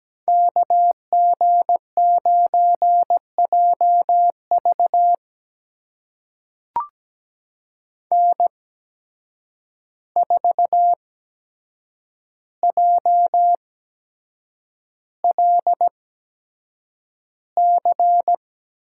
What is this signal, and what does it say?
Morse code, heard as a keyed tone, about 17 wpm, spaced as for 7 wpm, 700 Hz. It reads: KG9JV N4JLC
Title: USA Call Signs - 5 Characters - 8x Character Spacing - 17wpm